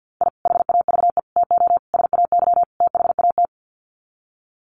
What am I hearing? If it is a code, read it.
Morse code, heard as a keyed tone, about 50 wpm, 700 Hz. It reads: I HAVE TO SAY THAT
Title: Sets of 5 Words Encore 50wpm